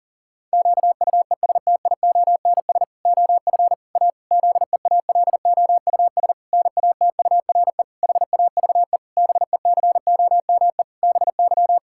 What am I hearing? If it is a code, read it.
Morse code, heard as a keyed tone, about 40 wpm, 700 Hz. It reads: QUESTIONS OF A ZEALOUS NATURE HAVE BECOME BY
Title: Pangram Challenge v2 - 40wpm